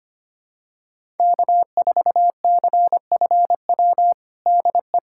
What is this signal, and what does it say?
Morse code, heard as a keyed tone, about 25 wpm, 700 Hz. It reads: K4CFW DE
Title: USA Callsign DE Callsign 25wpm